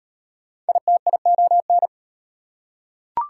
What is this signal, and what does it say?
Morse code, heard as a keyed tone, about 39 wpm, 700 Hz. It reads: ITION